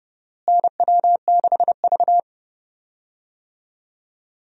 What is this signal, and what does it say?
Morse code, heard as a keyed tone, about 30 wpm, 700 Hz. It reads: NW6V
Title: USA Call Signs - 4 Characters - Repeated 3x - 30wpm